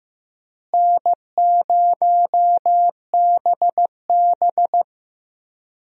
Morse code, heard as a keyed tone, about 15 wpm, 700 Hz. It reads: N0BB
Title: USA Call Signs - 4 Characters - Repeated 3x - 15wpm